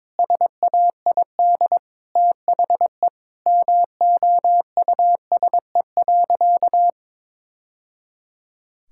Morse code, heard as a keyed tone, about 22 wpm, 700 Hz. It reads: SAID THE MOUSE.
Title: Sentences from Top 1000 Words - Review Part 5 - 22wpm